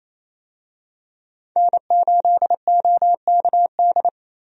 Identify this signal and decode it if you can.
Morse code, heard as a keyed tone, about 28 wpm, 700 Hz. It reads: N8OKD
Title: USA Call Signs - 5 Characters - Repeated 2x - 28wpm